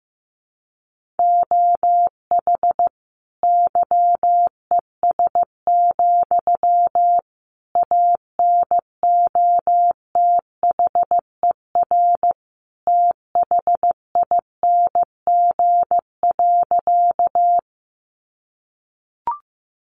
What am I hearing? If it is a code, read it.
Morse code, heard as a keyed tone, about 15 wpm, 700 Hz. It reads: OH YES, ANOTHER THING.
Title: Sentences from Top 600 Words 15wpm